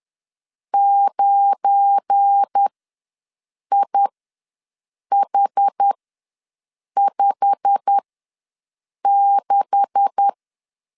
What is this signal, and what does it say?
Morse code, heard as a keyed tone, about 10 wpm, spaced as for 6 wpm, 800 Hz. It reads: 9IH56
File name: LEZIONI / 9 2867 ws 10wpm 4215